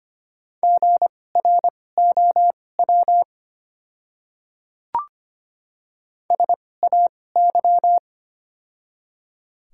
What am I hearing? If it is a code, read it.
Morse code, heard as a keyed tone, about 25 wpm, spaced as for 18 wpm, 690 Hz. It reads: GROW SAY
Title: 2x Character Spacing - Top 300 Words - 25wpm